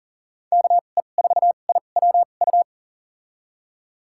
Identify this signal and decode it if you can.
Morse code, heard as a keyed tone, about 40 wpm, spaced as for 29 wpm, 700 Hz. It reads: KE4IWU
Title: USA Call Signs - 6 Characters - 2x Character Spacing - 40wpm